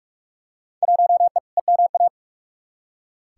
Morse code, heard as a keyed tone, about 44 wpm, 700 Hz. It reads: 1E EMA
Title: Field Day Exchanges Encore - 45wpm